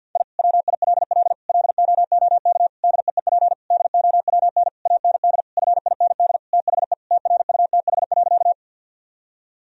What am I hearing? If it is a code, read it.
Morse code, heard as a keyed tone, about 50 wpm, 700 Hz. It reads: I WILL LOOK DEEP DOWN AND FIND THE TRUTH.
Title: Sentences from Top 1000 Words - Review Part 3 - 50wpm